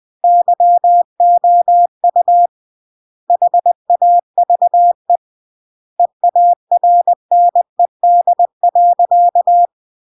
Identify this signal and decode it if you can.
Morse code, heard as a keyed tone, about 20 wpm, 700 Hz. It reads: YOU HAVE EARNED.
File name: CW / jodorowsky / 82 commandments 20-20